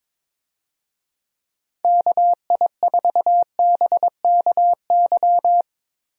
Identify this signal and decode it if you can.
Morse code, heard as a keyed tone, about 22 wpm, 700 Hz. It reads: KI4BKY
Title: USA Call Signs - 6 Characters - Repeated 2x - 22wpm